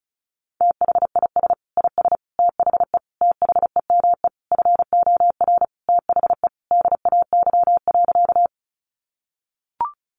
Morse code, heard as a keyed tone, about 34 wpm, 700 Hz. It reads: THIS IS THE THEME FOR THE DAY.
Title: Sentences from Top 1100 Words 35wpm